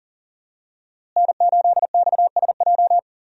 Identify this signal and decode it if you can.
Morse code, heard as a keyed tone, about 40 wpm, 700 Hz. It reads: N8XSJ
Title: USA Call Signs - 5 Characters - Repeated 3x - 40wpm